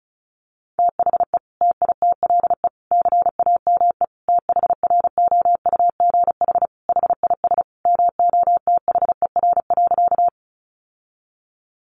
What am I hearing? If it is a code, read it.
Morse code, heard as a keyed tone, about 34 wpm, 700 Hz. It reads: THE TITLE CAME THROUGH HIS MOTHER.
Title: Sentences from Top 1200 Words 35wpm